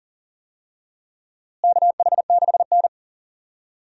Morse code, heard as a keyed tone, about 40 wpm, 700 Hz. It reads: KH6N
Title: DXCC-100 Call Signs - 4 Characters - Repeated 3x - 40wpm